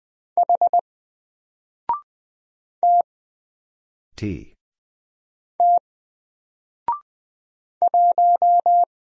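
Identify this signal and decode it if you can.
Morse code, heard as a keyed tone, about 20 wpm, 700 Hz. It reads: H T T 1
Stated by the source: Single letter and number Morse drill at 20 wpm, from N4LCD.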